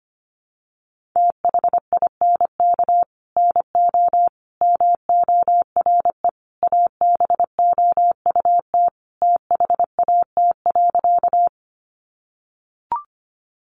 Morse code, heard as a keyed tone, about 25 wpm, 700 Hz. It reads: THINK NO MORE ABOUT THAT.
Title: Sentences from Top 200 Words 25wpm